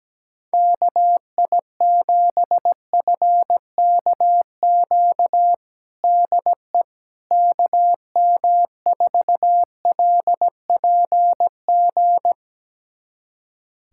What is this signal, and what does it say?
Morse code, heard as a keyed tone, about 17 wpm, 700 Hz. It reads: KI7FKQ DE KM4LPG
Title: USA Callsign DE Callsign - 6 Characters - 17wpm